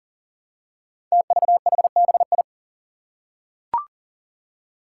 Morse code, heard as a keyed tone, about 40 wpm, 700 Hz. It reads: TVHBI
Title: Five Challenging Letter-Number Combinations 40wpm